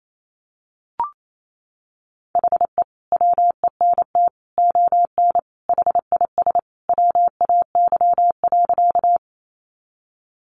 Morse code, heard as a keyed tone, about 28 wpm, 700 Hz. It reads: HE WENT ON HIS WAY.
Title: Sentences from Top 200 Words 28wpm